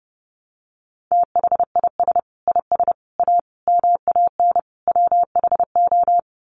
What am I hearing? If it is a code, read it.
Morse code, heard as a keyed tone, about 30 wpm, 700 Hz. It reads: THIS IS A MAN WHO
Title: Sets of 5 Words Encore2 30wpm